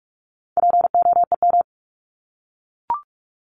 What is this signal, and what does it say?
Morse code, heard as a keyed tone, about 44 wpm, 700 Hz. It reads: POEM
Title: Top 1751-2000 Words 45wpm